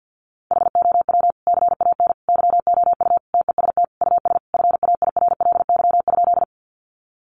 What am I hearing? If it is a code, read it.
Morse code, heard as a keyed tone, about 50 wpm, 700 Hz. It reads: HOW CAN YOU TEST US FAIRLY?
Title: Sentences from Top 1100 Words 50wpm